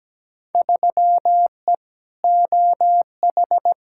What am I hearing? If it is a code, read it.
Morse code, heard as a keyed tone, about 17 wpm, 700 Hz. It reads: 3E OH
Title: Field Day Exchanges - 17wpm